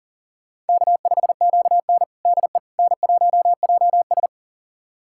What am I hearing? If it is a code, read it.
Morse code, heard as a keyed tone, about 40 wpm, 700 Hz. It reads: K5QN DE N1JS
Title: USA Callsign DE Callsign - 4 Characters - 40wpm